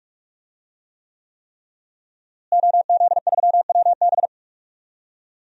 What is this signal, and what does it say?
Morse code, heard as a keyed tone, about 46 wpm, 700 Hz. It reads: OZ3WB
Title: DXCC-100 Call Signs - 5 Characters - Repeated 3x - 45wpm